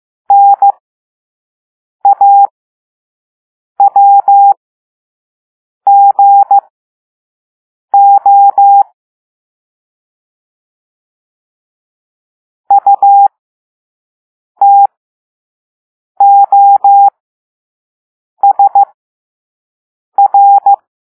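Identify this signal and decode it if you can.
Morse code, heard as a keyed tone, about 15 wpm, spaced as for 5 wpm, 800 Hz. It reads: NAWGO UTOSR
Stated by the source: Morse code lesson 5 drill of random five-letter groups using all letters learned so far.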